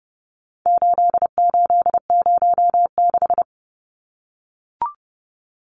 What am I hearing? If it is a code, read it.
Morse code, heard as a keyed tone, about 30 wpm, 700 Hz. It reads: 8806
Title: Numbers - 4-Digit 30wpm